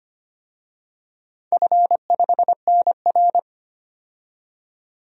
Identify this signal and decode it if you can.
Morse code, heard as a keyed tone, about 25 wpm, 700 Hz. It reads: F5NR
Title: DXCC-100 Call Signs - 4 Characters - 25wpm